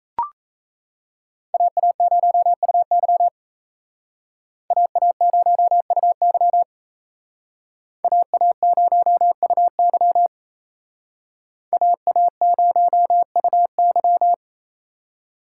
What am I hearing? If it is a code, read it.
Morse code, heard as a keyed tone, about 36 wpm, 700 Hz. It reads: AA0UY AA0UY AA0UY AA0UY